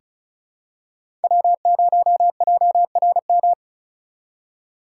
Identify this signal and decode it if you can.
Morse code, heard as a keyed tone, about 36 wpm, 700 Hz. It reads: W0JRM
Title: USA Call Signs - 5 Characters - 4x Character Spacing - 35wpm